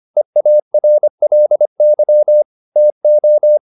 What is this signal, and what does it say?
Morse code, heard as a keyed tone, about 25 wpm, 600 Hz. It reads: EARLY TO
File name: the old man and the sea part 1 25wpm